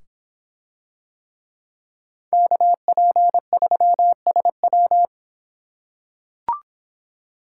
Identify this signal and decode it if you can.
Morse code, heard as a keyed tone, about 26 wpm, 700 Hz. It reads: KP3SW